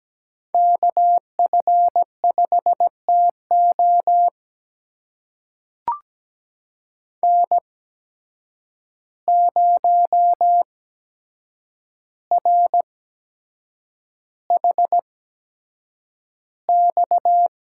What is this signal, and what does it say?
Morse code, heard as a keyed tone, about 17 wpm, spaced as for 7 wpm, 700 Hz. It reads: KF5TO N0RHX
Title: USA Call Signs - 5 Characters - 8x Character Spacing - 17wpm